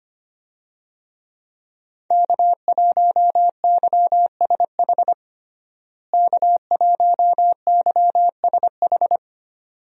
Morse code, heard as a keyed tone, about 25 wpm, 700 Hz. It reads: K1YSH K1YSH
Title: USA Call Signs - 5 Characters - Repeated 2x - 25wpm